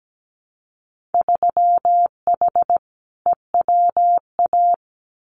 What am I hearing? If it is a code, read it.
Morse code, heard as a keyed tone, about 17 wpm, 700 Hz. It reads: 3H EWA
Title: Winter Field Day Exchanges Encore 17wpm